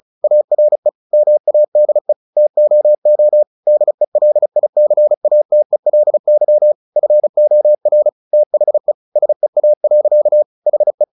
Morse code, heard as a keyed tone, about 34 wpm, 600 Hz. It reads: ARE MADE TOO DELICATELY FOR THE SEA. HE
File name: the old man and the sea / the old man and the sea 1 35wpm